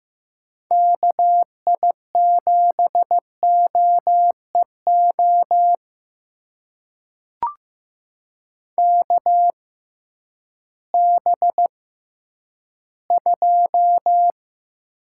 Morse code, heard as a keyed tone, about 15 wpm, spaced as for 10 wpm, 700 Hz. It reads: KI7OEO KB2
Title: USA Call Signs - 6 Characters - 6x Character Spacing - 15wpm